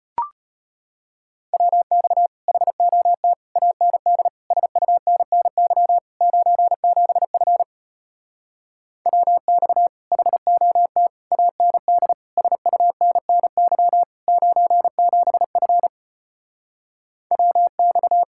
Morse code, heard as a keyed tone, about 33 wpm, 700 Hz. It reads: WX HOT AND SUNNY 97F WX HOT AND SUNNY 97F WX